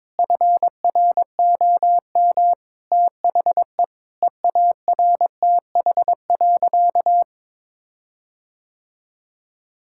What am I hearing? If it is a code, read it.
Morse code, heard as a keyed tone, about 22 wpm, 700 Hz. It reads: FROM THE EARTH.
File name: Sentences with 9-Word Phrases - 22wpm